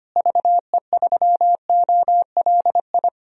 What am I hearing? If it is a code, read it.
Morse code, heard as a keyed tone, about 25 wpm, 700 Hz. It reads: VE3OLI
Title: DXCC-100 Call Signs - 6 Characters - Repeated 2x - 25wpm